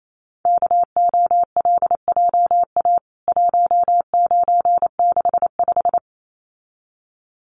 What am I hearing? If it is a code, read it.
Morse code, heard as a keyed tone, about 28 wpm, 700 Hz. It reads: KOLJA 1965
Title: CWT Exchange 28wpm